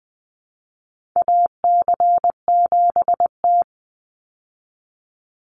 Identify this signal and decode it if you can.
Morse code, heard as a keyed tone, about 20 wpm, 700 Hz. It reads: AC7T
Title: Call Signs Repeated 2x - 20wpm